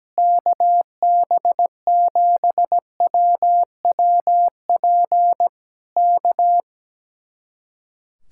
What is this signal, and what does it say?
Morse code, heard as a keyed tone, about 17 wpm, 700 Hz. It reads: KB7WWP K
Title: Speed-Racing Common QSO Phrases - 17wpm